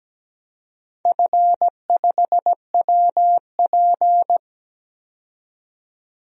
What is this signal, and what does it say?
Morse code, heard as a keyed tone, about 17 wpm, 700 Hz. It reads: F5WP
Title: DXCC-100 Call Signs - 4 Characters - 17wpm